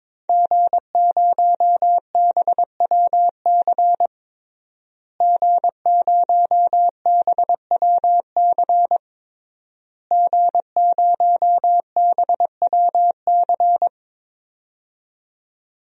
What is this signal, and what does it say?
Morse code, heard as a keyed tone, about 21 wpm, 700 Hz. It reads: G0BWC G0BWC G0BWC
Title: DXCC-100 Call Signs - 5 Characters - Repeated 3x - 22wpm